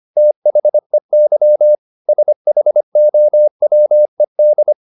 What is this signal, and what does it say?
Morse code, heard as a keyed tone, about 25 wpm, 600 Hz. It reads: THEY SHOWED
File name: the old man and the sea part 1 25wpm